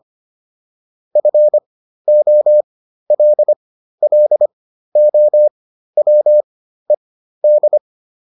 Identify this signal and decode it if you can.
Morse code, heard as a keyed tone, about 25 wpm, spaced as for 13 wpm, 600 Hz. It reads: FOLLOWED